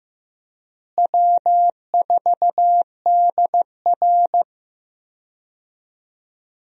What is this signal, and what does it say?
Morse code, heard as a keyed tone, about 15 wpm, 700 Hz. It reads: W4DR
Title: USA Call Signs - 4 Characters - Repeated 2x - 15wpm